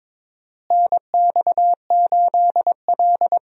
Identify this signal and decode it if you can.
Morse code, heard as a keyed tone, about 22 wpm, 700 Hz. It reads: NX8L
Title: USA Call Signs - 4 Characters - Repeated 2x - 22wpm